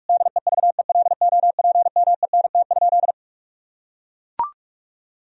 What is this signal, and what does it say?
Morse code, heard as a keyed tone, about 41 wpm, 700 Hz. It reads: DEVELOPMENT?
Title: Sentences from Top 1000 Words - Review Part 4 - 45wpm